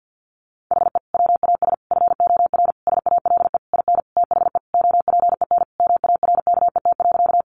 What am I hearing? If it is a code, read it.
Morse code, heard as a keyed tone, about 50 wpm, 700 Hz. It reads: HE WAS FOR SALE IN THE OPEN MARKET.
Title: Sentences from Top 1400 Words 50wpm